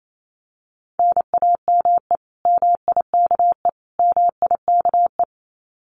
Morse code, heard as a keyed tone, about 28 wpm, 700 Hz. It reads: NAME MIKE MIKE
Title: Common QSO Phrases Encore 28wpm